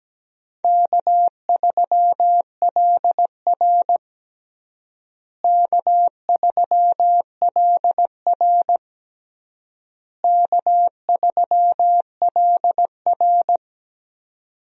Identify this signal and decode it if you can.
Morse code, heard as a keyed tone, about 17 wpm, 700 Hz. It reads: K3LR K3LR K3LR
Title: USA Call Signs - 4 Characters - Repeated 3x - 17wpm